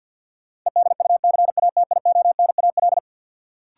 Morse code, heard as a keyed tone, about 53 wpm, 700 Hz. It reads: EDUCATIONAL
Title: Sentences from Top 1200 Words - New Word Drill - 50wpm